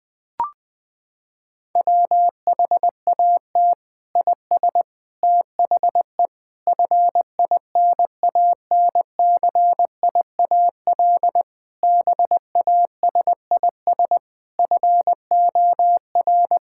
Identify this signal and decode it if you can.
Morse code, heard as a keyed tone, about 20 wpm, 700 Hz. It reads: WHAT IS THE FINANCIAL BASIS FOR
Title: Sentences from Top 1000 Words - Review Part 5 - 20wpm